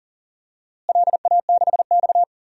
Morse code, heard as a keyed tone, about 37 wpm, 700 Hz. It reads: LA6X
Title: DXCC-100 Call Signs - 4 Characters - Repeated 2x - 40wpm